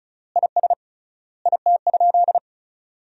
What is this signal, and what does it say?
Morse code, heard as a keyed tone, about 34 wpm, 700 Hz. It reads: IS IT?
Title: 3x Word-Spacing - Sentences from Top 500 Words - 35wpm